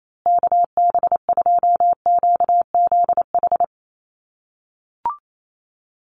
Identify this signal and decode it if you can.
Morse code, heard as a keyed tone, about 28 wpm, 700 Hz. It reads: KB2QZH
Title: USA Call Signs Repeated 2x 28wpm